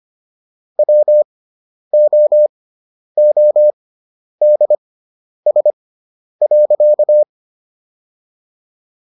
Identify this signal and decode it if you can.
Morse code, heard as a keyed tone, about 25 wpm, spaced as for 10 wpm, 600 Hz. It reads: WOODS.